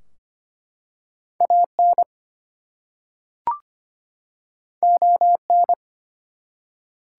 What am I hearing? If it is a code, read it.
Morse code, heard as a keyed tone, about 25 wpm, 700 Hz. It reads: AN ON
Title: Two Letter Words - 25wpm